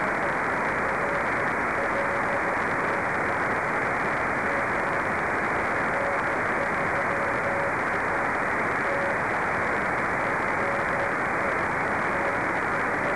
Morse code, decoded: none